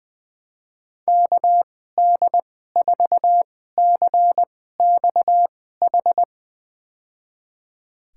Morse code, heard as a keyed tone, about 20 wpm, spaced as for 14 wpm, 700 Hz. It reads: KD4CXH
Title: USA Call Signs - 6 Characters - 2x Character Spacing - 20wpm